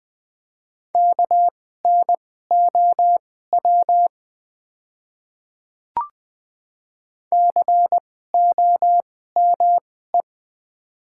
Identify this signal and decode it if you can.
Morse code, heard as a keyed tone, about 20 wpm, spaced as for 14 wpm, 700 Hz. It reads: KNOW COME
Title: 2x Character Spacing - Top 100 Words - 20wpm